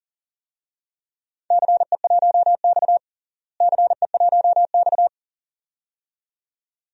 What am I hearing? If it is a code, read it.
Morse code, heard as a keyed tone, about 40 wpm, 700 Hz. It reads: CE1X CE1X
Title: DXCC-100 Call Signs - 4 Characters - Repeated 2x - 40wpm